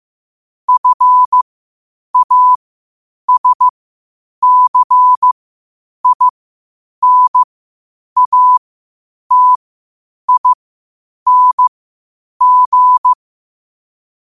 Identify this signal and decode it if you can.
Morse code, heard as a keyed tone, about 15 wpm, spaced as for 8 wpm, 1000 Hz. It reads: FASCINATING
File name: qotdfiles / 8wpm